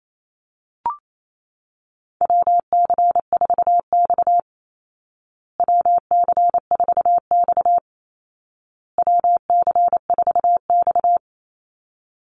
Morse code, heard as a keyed tone, about 28 wpm, 700 Hz. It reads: WC4X WC4X WC4X
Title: Call Signs Repeated 2x - 28wpm